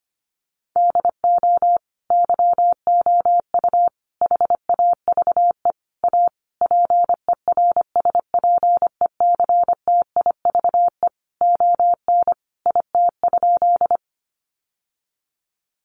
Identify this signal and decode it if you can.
Morse code, heard as a keyed tone, about 25 wpm, 700 Hz. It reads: DO YOU HAVE A PERSPECTIVE ON IT?
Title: Sentences from Top 1200 Words 25wpm